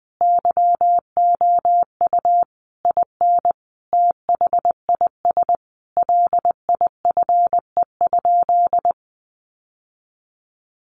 Morse code, heard as a keyed tone, about 20 wpm, 700 Hz. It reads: YOU IN THIS LIFE?
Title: Sentences from Top 1200 Words 20wpm